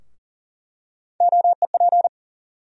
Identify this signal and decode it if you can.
Morse code, heard as a keyed tone, about 40 wpm, 700 Hz. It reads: OEP
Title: USA Call Sign Postfixes - 3 Letters 40wpm